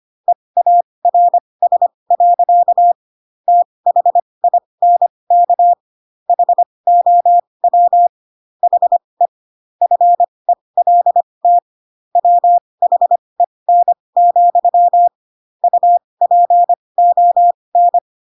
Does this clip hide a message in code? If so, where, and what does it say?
Yes: Morse code from 0.3 to 18.0 s, heard as a keyed tone, about 25 wpm, spaced as for 20 wpm, 700 Hz. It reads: EARS. THINK HOW HE FELT WHEN, UPON